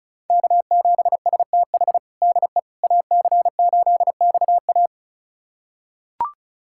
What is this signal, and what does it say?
Morse code, heard as a keyed tone, about 34 wpm, 700 Hz. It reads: K7STH DE AC8XA
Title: USA Callsign DE Callsign - 5 Characters - 35wpm